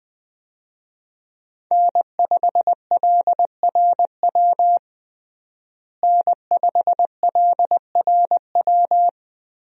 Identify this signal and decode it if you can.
Morse code, heard as a keyed tone, about 20 wpm, 700 Hz. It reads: N5LRW N5LRW
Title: USA Call Signs - 5 Characters - Repeated 2x - 20wpm